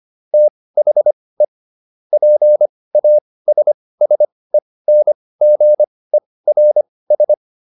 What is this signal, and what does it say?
Morse code, heard as a keyed tone, about 25 wpm, spaced as for 18 wpm, 600 Hz. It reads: THE PASSENGERS